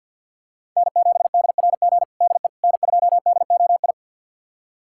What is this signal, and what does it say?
Morse code, heard as a keyed tone, about 48 wpm, 700 Hz. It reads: N7DRG DE N2DOI
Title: USA Callsign DE Callsign - 5 Characters - 50wpm